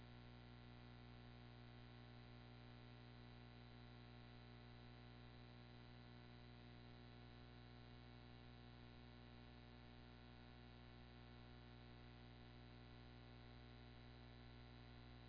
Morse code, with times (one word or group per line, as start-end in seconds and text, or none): none